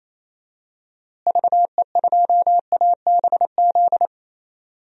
Morse code, heard as a keyed tone, about 28 wpm, 700 Hz. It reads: VE2ABZ
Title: DXCC-100 Call Signs - 6 Characters - Repeated 2x - 28wpm